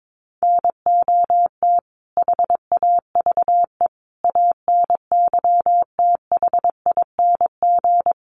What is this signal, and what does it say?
Morse code, heard as a keyed tone, about 22 wpm, 700 Hz. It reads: NOT HAVE ANYTHING
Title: Sets of 5 Words Encore 22wpm